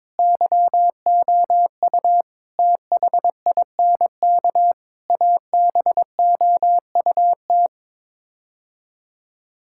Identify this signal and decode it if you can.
Morse code, heard as a keyed tone, about 22 wpm, 700 Hz. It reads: YOU THINK ABOUT
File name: practice / 22wpm / Sets of 8 Words - 22wpm